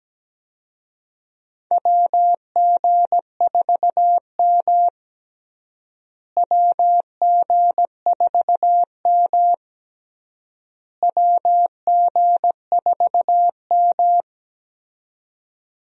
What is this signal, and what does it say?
Morse code, heard as a keyed tone, about 17 wpm, 700 Hz. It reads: WG4M WG4M WG4M
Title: USA Call Signs - 4 Characters - Repeated 3x - 17wpm